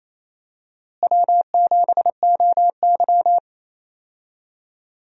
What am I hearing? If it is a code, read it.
Morse code, heard as a keyed tone, about 28 wpm, 700 Hz. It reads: W7OY
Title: USA Call Signs - 4 Characters - Repeated 3x - 28wpm